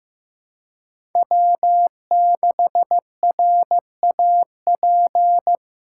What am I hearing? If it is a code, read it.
Morse code, heard as a keyed tone, about 15 wpm, 700 Hz. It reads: W6RAP
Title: USA Callsign DE Callsign - 5 Characters - 15wpm